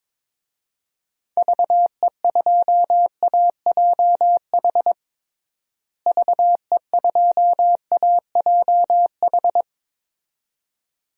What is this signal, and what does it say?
Morse code, heard as a keyed tone, about 22 wpm, 700 Hz. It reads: VE2AJH VE2AJH
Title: DXCC-100 Call Signs - 6 Characters - Repeated 2x - 22wpm